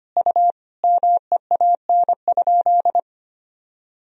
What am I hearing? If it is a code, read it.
Morse code, heard as a keyed tone, about 25 wpm, 700 Hz. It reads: U MEAN?
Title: Sentences from Top 200 Words and Abbreviations 25wpm